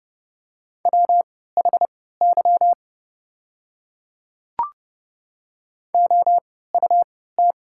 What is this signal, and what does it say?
Morse code, heard as a keyed tone, about 29 wpm, spaced as for 17 wpm, 690 Hz. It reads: WHY OUT
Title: 3x Character Spacing - Top 300 Words - 30wpm